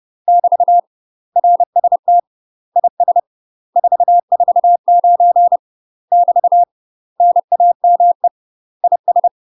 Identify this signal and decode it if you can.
Morse code, heard as a keyed tone, about 30 wpm, 700 Hz. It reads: <BT> RST IS 449 <BT> NAME IS
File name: UK MASTER QSOs TRAINING / 30WPM / UK MASTER QSOs 0040